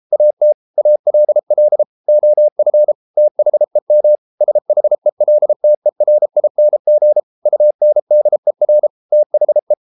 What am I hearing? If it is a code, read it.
Morse code, heard as a keyed tone, about 33 wpm, 600 Hz. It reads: AT ALL OF THEM SHELTERING UNDER THE